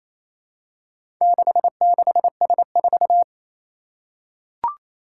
Morse code, heard as a keyed tone, about 28 wpm, 700 Hz. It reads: 66S4
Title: Four Challenging Letter-Number Combinations 28wpm